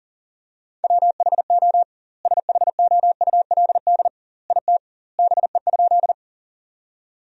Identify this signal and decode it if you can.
Morse code, heard as a keyed tone, about 40 wpm, 700 Hz. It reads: WHO SHOULD IT BE?
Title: 2x Word-Spacing - Sentences from Top 300 Words - 40wpm